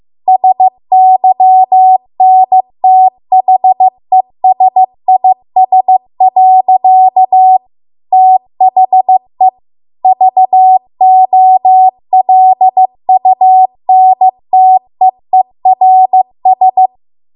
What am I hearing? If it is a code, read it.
Morse code, heard as a keyed tone, about 15 wpm, 750 Hz. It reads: SYNTHESIS. THE VOLUNTEERS